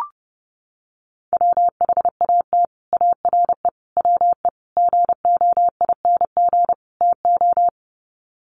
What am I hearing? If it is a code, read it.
Morse code, heard as a keyed tone, about 29 wpm, 700 Hz. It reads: WHAT ARE WE GOING TO
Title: Sets of 5 Words 30wpm